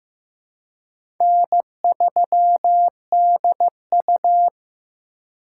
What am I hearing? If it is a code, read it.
Morse code, heard as a keyed tone, about 15 wpm, 700 Hz. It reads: N3DU
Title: USA Call Signs - 4 Characters - 3x Character Spacing - 15wpm